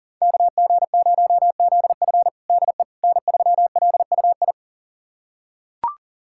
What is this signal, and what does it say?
Morse code, heard as a keyed tone, about 40 wpm, 700 Hz. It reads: KG0ZF DE N3LUI
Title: USA Callsign DE Callsign - 5 Characters - 40wpm